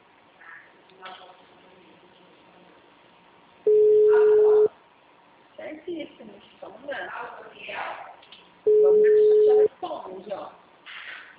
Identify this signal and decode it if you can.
Ringback tone (Europe, 425 Hz), with speech.